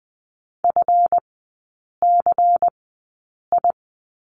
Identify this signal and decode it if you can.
Morse code, heard as a keyed tone, about 20 wpm, spaced as for 8 wpm, 700 Hz. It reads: FCI
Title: 138 Lesson 23e - Letter f - Three Characters 20wpm